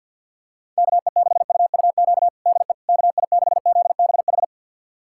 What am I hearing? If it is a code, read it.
Morse code, heard as a keyed tone, about 50 wpm, 700 Hz. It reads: KE6UUQ DE KI6ZBH
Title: USA Callsign DE Callsign 50wpm